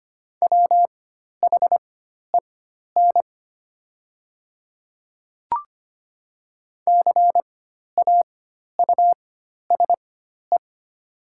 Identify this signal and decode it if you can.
Morse code, heard as a keyed tone, about 25 wpm, spaced as for 12 wpm, 700 Hz. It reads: WHEN CAUSE
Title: 4x Character Spacing - Top 200 Words - 25wpm